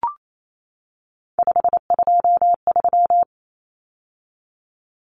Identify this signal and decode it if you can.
Morse code, heard as a keyed tone, about 28 wpm, 700 Hz. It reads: 523
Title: Numbers - 3-Digit 28wpm